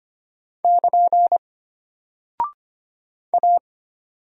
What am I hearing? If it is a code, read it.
Morse code, heard as a keyed tone, about 25 wpm, 700 Hz. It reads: <KN> A
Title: Single Character (Letter:Number:Punctuation:Prosign) Alt 25wpm